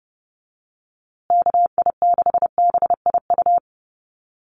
Morse code, heard as a keyed tone, about 30 wpm, 700 Hz. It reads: KI6BIU
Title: USA Call Signs Repeated 3x 30wpm